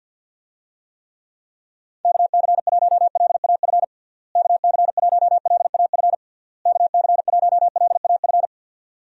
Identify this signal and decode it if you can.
Morse code, heard as a keyed tone, about 48 wpm, 700 Hz. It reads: KC1LAF KC1LAF KC1LAF
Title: USA Call Signs - 6 Characters - Repeated 3x - 50wpm